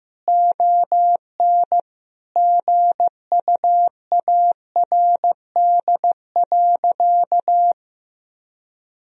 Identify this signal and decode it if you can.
Morse code, heard as a keyed tone, about 15 wpm, 700 Hz. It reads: ON GUARD.
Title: Sentences from Top 1000 Words - Review Part 4 - 15wpm